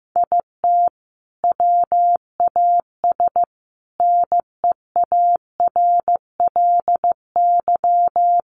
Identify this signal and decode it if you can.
Morse code, heard as a keyed tone, about 15 wpm, 700 Hz. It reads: IT WAS NEARLY
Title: Sentences from Top 1400 Words 15wpm